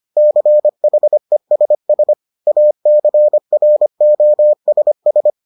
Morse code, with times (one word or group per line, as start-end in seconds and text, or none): 0.2-2.1 CHESS
2.5-5.3 ACROSS